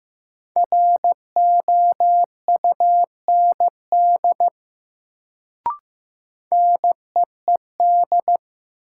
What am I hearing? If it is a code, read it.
Morse code, heard as a keyed tone, about 15 wpm, 700 Hz. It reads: ROUND NEED
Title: Sentences from Top 200 Words - New Word Drill - 15wpm